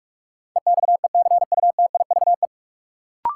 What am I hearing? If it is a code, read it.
Morse code, heard as a keyed tone, about 46 wpm, 700 Hz. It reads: EXECUTIVE
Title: Sets of 5 Words Encore - New Word Drill - 45wpm